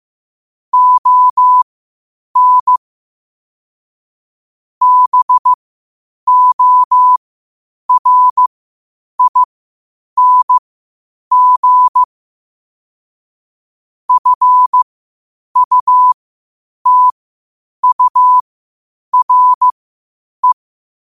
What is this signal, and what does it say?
Morse code, heard as a keyed tone, about 15 wpm, spaced as for 8 wpm, 1000 Hz. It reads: ON BORING FUTURE
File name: qotdfiles / 8wpm